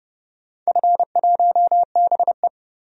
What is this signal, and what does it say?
Morse code, heard as a keyed tone, about 30 wpm, 700 Hz. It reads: F1BE
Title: DXCC-100 Call Signs - 4 Characters - 30wpm